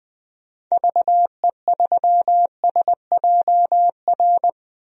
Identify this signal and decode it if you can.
Morse code, heard as a keyed tone, about 20 wpm, 700 Hz. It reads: VE3SJR
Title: DXCC-100 Call Signs - 6 Characters - 20wpm